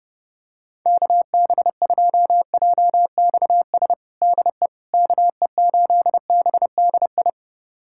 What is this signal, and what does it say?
Morse code, heard as a keyed tone, about 30 wpm, 700 Hz. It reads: KB2JXS DE KE8BDI
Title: USA Callsign DE Callsign 30wpm